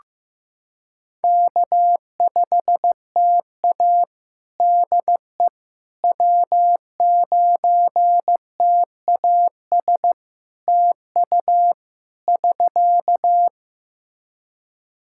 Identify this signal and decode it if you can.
Morse code, heard as a keyed tone, about 14 wpm, 700 Hz. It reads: K5TA DE W9TAS TU <SK>
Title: Practice QSO - Exchange 4 - Ending - 15wpm